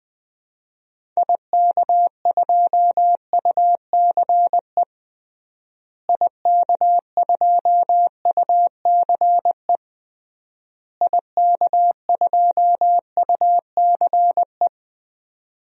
Morse code, heard as a keyed tone, about 20 wpm, 700 Hz. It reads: IK2UCE IK2UCE IK2UCE
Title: DXCC-100 Call Signs - 6 Characters - Repeated 3x - 20wpm